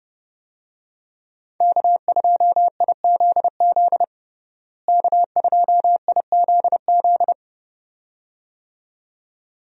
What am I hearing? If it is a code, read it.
Morse code, heard as a keyed tone, about 30 wpm, 700 Hz. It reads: K2IZZ K2IZZ
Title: USA Call Signs - 5 Characters - Repeated 2x - 30wpm